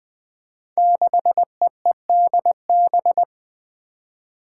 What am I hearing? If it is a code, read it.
Morse code, heard as a keyed tone, about 20 wpm, 700 Hz. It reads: 6EEDB
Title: Five Challenging Letter-Number Combinations 20wpm